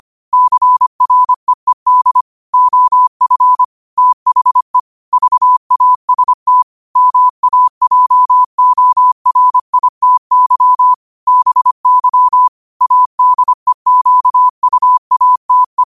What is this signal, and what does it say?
Morse code, heard as a keyed tone, about 25 wpm, 1000 Hz. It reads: CREED OF THE VAST MAJORITY BY ADEQUATE
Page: New quotes every day in morse code at 25 Words per minute.